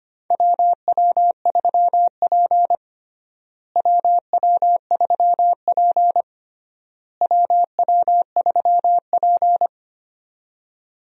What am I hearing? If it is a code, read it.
Morse code, heard as a keyed tone, about 25 wpm, 700 Hz. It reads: WW3P WW3P WW3P
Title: USA Call Signs - 4 Characters - Repeated 3x - 25wpm